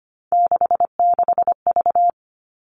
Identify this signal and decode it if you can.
Morse code, heard as a keyed tone, about 25 wpm, 700 Hz. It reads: 66V
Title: All Cut Numbers - 3-Digit 25wpm